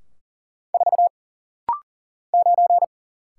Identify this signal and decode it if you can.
Morse code, heard as a keyed tone, about 40 wpm, 700 Hz. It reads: 4 9
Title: Single Letter-Number - Rapid-Fire 40wpm